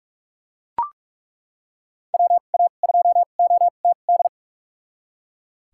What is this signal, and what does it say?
Morse code, heard as a keyed tone, about 41 wpm, spaced as for 31 wpm, 700 Hz. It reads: WA2OTD
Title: USA Call Signs - 6 Characters - 2x Character Spacing - 45wpm